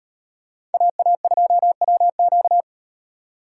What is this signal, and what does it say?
Morse code, heard as a keyed tone, about 38 wpm, 700 Hz. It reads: AA2WQ